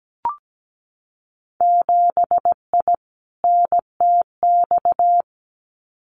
Morse code, heard as a keyed tone, about 17 wpm, 700 Hz. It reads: 7I NTX
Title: Winter Field Day Exchanges Encore 17wpm